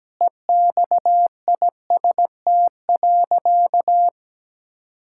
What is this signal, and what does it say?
Morse code, heard as a keyed tone, about 17 wpm, 700 Hz. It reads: EXIST.
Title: Sentences from Top 1000 Words - Review Part 5 - 17wpm